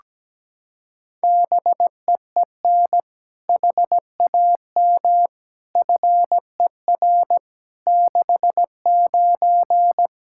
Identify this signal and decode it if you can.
Morse code, heard as a keyed tone, about 17 wpm, 700 Hz. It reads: BEEN HAM FER 69
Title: Practice QSO - Exchange 3 - More Info - 17wpm